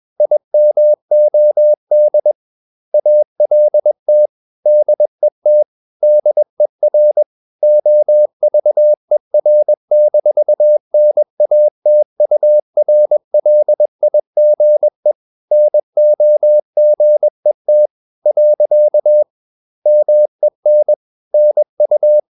Morse code, heard as a keyed tone, about 21 wpm, 600 Hz. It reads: IMOD ALT DET DER OVER-NATURLIGE NOGET. MEN NU
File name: Ferie QRS 21wpm